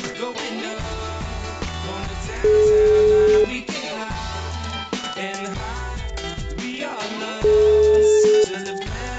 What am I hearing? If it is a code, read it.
Ringback tone (Europe, 425 Hz).